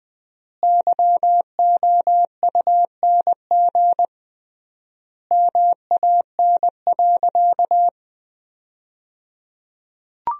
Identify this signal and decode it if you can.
Morse code, heard as a keyed tone, about 21 wpm, 700 Hz. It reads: YOUNG MAN.
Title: 3x Word-Spacing - Sentences from Top 400 Words - 20wpm